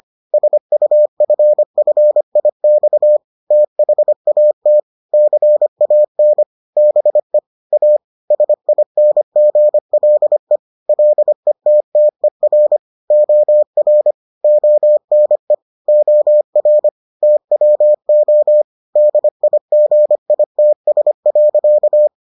Morse code, heard as a keyed tone, about 25 wpm, 600 Hz. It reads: SUFFIX THAT CAN BE A SINGLE LETTER OR ONE OR TWO DIGITS.